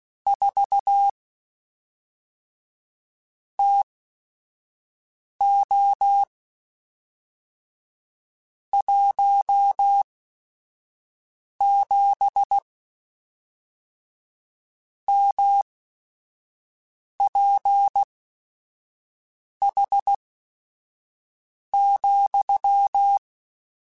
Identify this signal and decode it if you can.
Morse code, heard as a keyed tone, about 16 wpm, spaced as for 5 wpm, 800 Hz. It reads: 4 TO 17 MPH,
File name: Weather Sets / Weather Sets-mp3s / Weather 4 21.@5